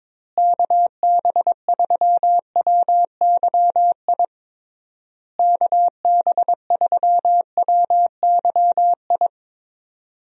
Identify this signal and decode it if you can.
Morse code, heard as a keyed tone, about 22 wpm, 700 Hz. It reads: KB3WYI KB3WYI
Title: USA Call Signs - 6 Characters - Repeated 3x - 22wpm